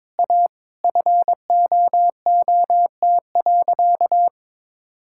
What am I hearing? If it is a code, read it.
Morse code, heard as a keyed tone, about 22 wpm, 700 Hz. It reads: A FOOT.
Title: Sentences from Top 1600 Words 22wpm